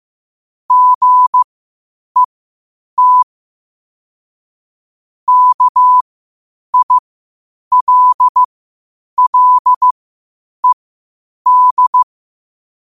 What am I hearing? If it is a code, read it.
Morse code, heard as a keyed tone, about 15 wpm, spaced as for 8 wpm, 1000 Hz. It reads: GET KILLED
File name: qotdfiles / 8wpm